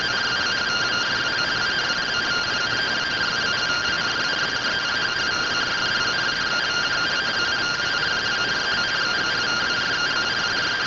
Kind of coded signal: two-tone teleprinter signal

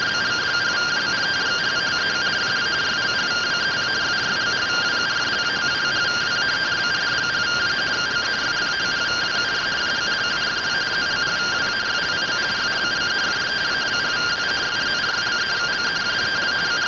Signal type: two-tone teleprinter signal

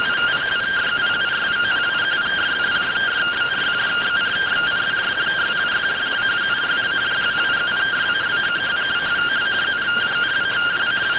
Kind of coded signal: two-tone teleprinter signal

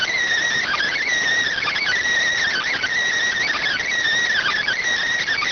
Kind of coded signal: Emergency Alert System (SAME) tones

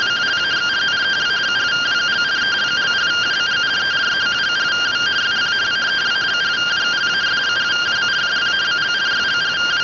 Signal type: two-tone teleprinter signal

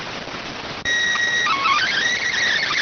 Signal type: SSTV (slow-scan television)

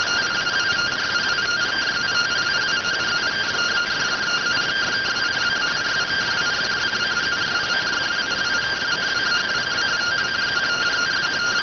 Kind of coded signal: two-tone teleprinter signal

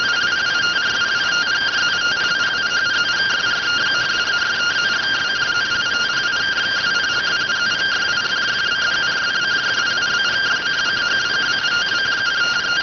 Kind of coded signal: two-tone teleprinter signal

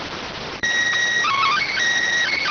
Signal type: SSTV (slow-scan television)

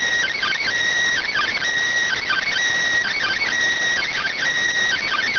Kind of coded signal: Emergency Alert System (SAME) tones

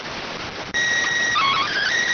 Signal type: SSTV (slow-scan television)

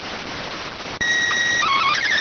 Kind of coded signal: SSTV (slow-scan television)

